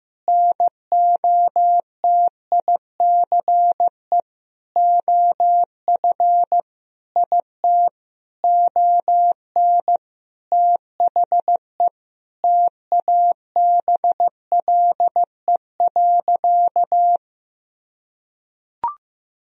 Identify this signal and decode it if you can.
Morse code, heard as a keyed tone, about 15 wpm, 700 Hz. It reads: NOTICE OF IT ON THE TABLE.
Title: Sentences from Top 1000 Words - Review Part 2 - 15wpm